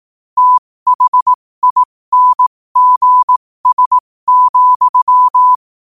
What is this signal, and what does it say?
Morse code, heard as a keyed tone, about 18 wpm, spaced as for 15 wpm, 1000 Hz. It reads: THINGS,
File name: qotdfiles / 15wpm